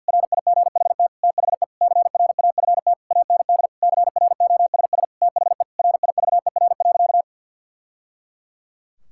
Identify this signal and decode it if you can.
Morse code, heard as a keyed tone, about 50 wpm, 700 Hz. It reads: RIGHT THE CRAFT AND CROSS THE RIVER.
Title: Sentences from Top 1000 Words - Review Part 6 - 50wpm